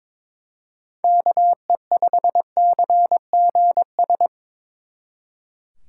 Morse code, heard as a keyed tone, about 25 wpm, 700 Hz. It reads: KE5CGS
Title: Speed Racing - USA 2x3 Call Signs 22wpm